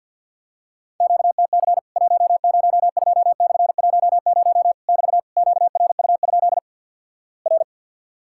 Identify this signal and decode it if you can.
Morse code, heard as a keyed tone, about 50 wpm, 700 Hz. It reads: QTC 102/10 <BT> QRU? R
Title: Worked All Europe DX Contest - QTC Exchange - 50wpm